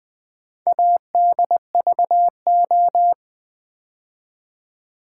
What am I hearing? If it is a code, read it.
Morse code, heard as a keyed tone, about 20 wpm, 700 Hz. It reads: ADVO
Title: Build A Word - 2 Letters At A Time - 8-Letter Words - 20wpm